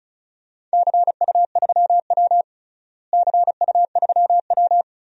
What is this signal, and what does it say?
Morse code, heard as a keyed tone, about 34 wpm, 700 Hz. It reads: CU3W CU3W
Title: DXCC-100 Call Signs - 4 Characters - Repeated 3x - 35wpm